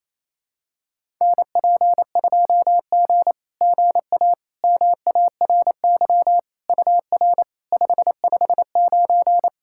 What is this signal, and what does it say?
Morse code, heard as a keyed tone, about 28 wpm, 700 Hz. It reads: NP2G GA MARY UR 559
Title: SOTA QSO Practice 28wpm